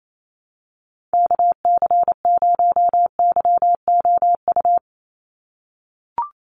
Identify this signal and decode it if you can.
Morse code, heard as a keyed tone, about 28 wpm, 700 Hz. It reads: KC0YOU
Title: USA Call Signs 28wpm